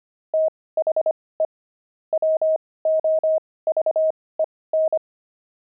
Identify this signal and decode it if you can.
Morse code, heard as a keyed tone, about 25 wpm, spaced as for 18 wpm, 630 Hz. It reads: THE WOVEN
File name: pangram-fs-18-cs-25-sixty